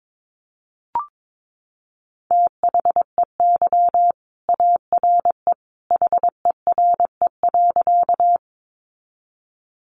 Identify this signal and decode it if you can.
Morse code, heard as a keyed tone, about 22 wpm, 700 Hz. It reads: THEY ARE HERE.
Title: Sentences from Top 200 Words 22wpm